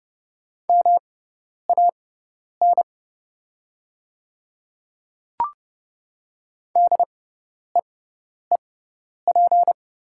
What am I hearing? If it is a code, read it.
Morse code, heard as a keyed tone, about 31 wpm, spaced as for 10 wpm, 700 Hz. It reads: MAN DEEP
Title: 6x Character Spacing - Top 500 Words - 30wpm